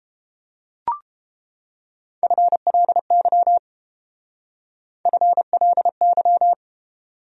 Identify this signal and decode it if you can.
Morse code, heard as a keyed tone, about 29 wpm, 700 Hz. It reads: FLY FLY